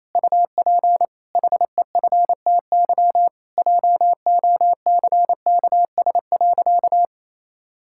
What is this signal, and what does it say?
Morse code, heard as a keyed tone, about 28 wpm, 700 Hz. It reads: UP HEFTY JOCKS.
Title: Pangram Challenge v2 - 28wpm